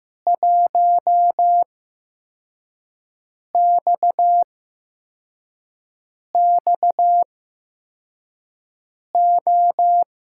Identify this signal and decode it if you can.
Morse code, heard as a keyed tone, about 15 wpm, spaced as for 4 wpm, 700 Hz. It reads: 1XXO
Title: USA Call Signs - 5 Characters - 8x Character Spacing - 15wpm